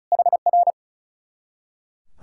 Morse code, heard as a keyed tone, about 44 wpm, 700 Hz. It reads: HR